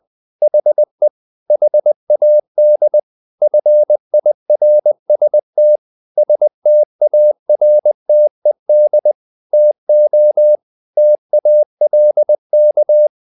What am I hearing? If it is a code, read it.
Morse code, heard as a keyed tone, about 20 wpm, 600 Hz. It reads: HE HAD FIRST STARTED TO TALK
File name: the old man and the sea / the old man and the sea part 1 20wpm